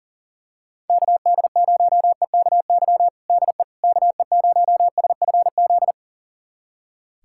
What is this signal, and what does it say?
Morse code, heard as a keyed tone, about 40 wpm, 700 Hz. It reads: KD0EKY DE KE0SFZ
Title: USA Callsign DE Callsign - 6 Characters - 40wpm